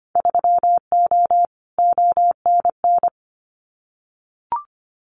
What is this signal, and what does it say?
Morse code, heard as a keyed tone, about 25 wpm, 700 Hz. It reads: 3O ONN
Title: Winter Field Day Exchanges 25wpm